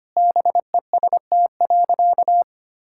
Morse code, heard as a keyed tone, about 25 wpm, 700 Hz. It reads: BEST.
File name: practice / 25wpm / Sentences with 2-Word Phrases - 25wpm